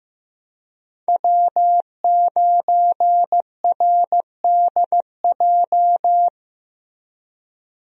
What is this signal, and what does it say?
Morse code, heard as a keyed tone, about 15 wpm, 700 Hz. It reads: W9RDJ
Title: USA Call Signs - 5 Characters - Repeated 2x - 15wpm